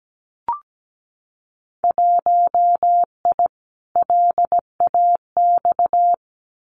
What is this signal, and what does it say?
Morse code, heard as a keyed tone, about 17 wpm, 700 Hz. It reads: 1I LAX
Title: Winter Field Day Exchanges 17wpm